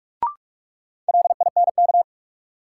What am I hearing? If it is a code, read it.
Morse code, heard as a keyed tone, about 44 wpm, 700 Hz. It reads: LINK
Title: Sentences from Top 1000 Words - Review Part 7 - New Word Drill - 45wpm